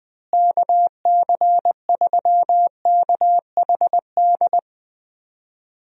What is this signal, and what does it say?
Morse code, heard as a keyed tone, about 21 wpm, 700 Hz. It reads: KC3KHD
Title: USA Call Signs - 6 Characters - Repeated 2x - 20wpm